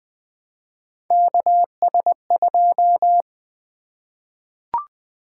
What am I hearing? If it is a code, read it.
Morse code, heard as a keyed tone, about 20 wpm, 700 Hz. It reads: KS2
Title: Three Letter-Number Combinations - 20wpm